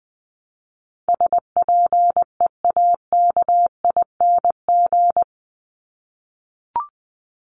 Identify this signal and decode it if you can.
Morse code, heard as a keyed tone, about 20 wpm, 700 Hz. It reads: SPEAKING
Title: ING Suffix - Top 100 Words with Base Words 20wpm